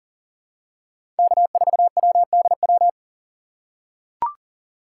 Morse code, heard as a keyed tone, about 40 wpm, 700 Hz. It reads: K4WDW
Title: USA Call Signs - 5 Characters - 2x Character Spacing - 40wpm